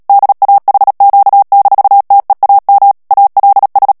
Morse code, heard as a keyed tone, about 37 wpm, 800 Hz. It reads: DAHQ-TEAM ALS
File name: morse mp3 / ALLEMAND texte clair 2 vitesse 35 mots minute